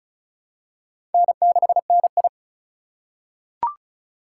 Morse code, heard as a keyed tone, about 34 wpm, 700 Hz. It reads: N6NI
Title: USA Call Signs - 4x Character Spacing - 35wpm